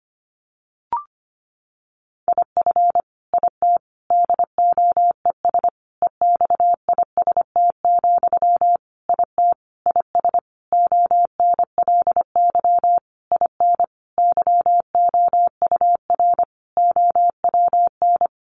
Morse code, heard as a keyed tone, about 25 wpm, 700 Hz. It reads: IF IT DOES EXIST, IT IS ONLY IN YOUR OWN
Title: Sentences from Top 1300 Words 25wpm